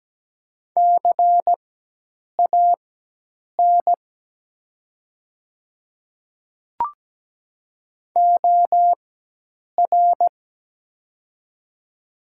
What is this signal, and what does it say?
Morse code, heard as a keyed tone, about 17 wpm, spaced as for 8 wpm, 700 Hz. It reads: CAN OR
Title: 4x Character Spacing - Top 200 Words - 17wpm